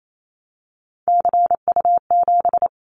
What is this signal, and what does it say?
Morse code, heard as a keyed tone, about 28 wpm, 700 Hz. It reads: CU7
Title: Callsign Prefixes for Top 50 DXCC - 28wpm